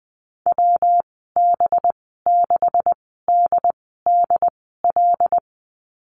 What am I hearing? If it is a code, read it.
Morse code, heard as a keyed tone, about 20 wpm, spaced as for 14 wpm, 700 Hz. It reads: WB6DDL
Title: 231 Lesson 37g - Letter x - Call Signs 20wpm